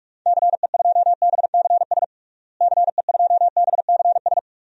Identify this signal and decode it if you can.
Morse code, heard as a keyed tone, about 41 wpm, 700 Hz. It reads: CE2BCS CE2BCS
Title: DXCC-100 Call Signs - 6 Characters - Repeated 2x - 45wpm